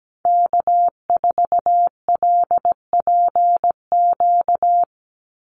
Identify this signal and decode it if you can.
Morse code, heard as a keyed tone, about 17 wpm, 700 Hz. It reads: K4LPQ
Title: NA SOTA Activator Callsigns 17wpm